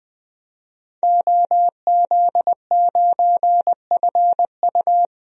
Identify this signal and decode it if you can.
Morse code, heard as a keyed tone, about 20 wpm, 700 Hz. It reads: OZ9FU
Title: DXCC-100 Call Signs - 5 Characters - 20wpm